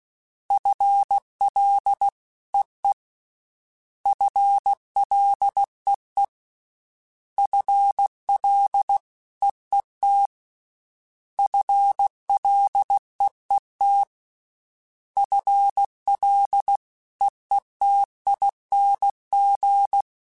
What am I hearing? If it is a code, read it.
Morse code, heard as a keyed tone, about 16 wpm, 800 Hz. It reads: FLEE FLEE FLEET FLEET FLEETING
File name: FL / FL mp3s / FL Prefixes 3x2 sp @15